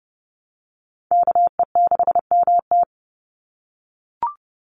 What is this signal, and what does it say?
Morse code, heard as a keyed tone, about 30 wpm, 700 Hz. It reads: KE6MT
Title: NA SOTA Chaser Callsigns 30wpm